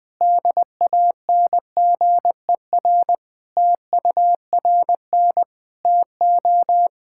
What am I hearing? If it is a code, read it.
Morse code, heard as a keyed tone, about 20 wpm, 700 Hz. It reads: DANGER TURN TO
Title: Sentences with 3-Word Phrases - 20wpm